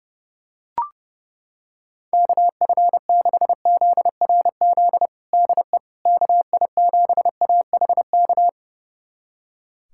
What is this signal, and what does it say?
Morse code, heard as a keyed tone, about 30 wpm, 700 Hz. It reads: KF6ZRZ DE KI7AHK
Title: USA Callsign DE Callsign 30wpm